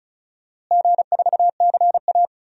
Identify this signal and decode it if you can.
Morse code, heard as a keyed tone, about 36 wpm, 700 Hz. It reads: G4CA
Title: DXCC-100 Call Signs - 4 Characters - Repeated 3x - 35wpm